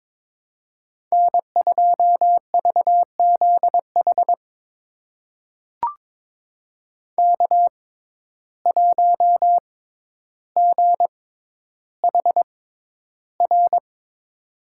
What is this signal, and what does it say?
Morse code, heard as a keyed tone, about 22 wpm, spaced as for 11 wpm, 700 Hz. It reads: N2VZH K1GHR
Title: USA Call Signs - 5 Characters - 6x Character Spacing - 22wpm